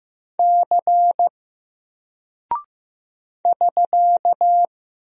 Morse code, heard as a keyed tone, about 15 wpm, 700 Hz. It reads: C <SK>
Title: Single Character (Letter:Number:Punctuation:Prosign) 15wpm